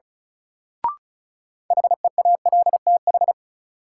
Morse code, heard as a keyed tone, about 36 wpm, 700 Hz. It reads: HEALTH
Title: Sets of 5 Words - New Word Drill - 35wpm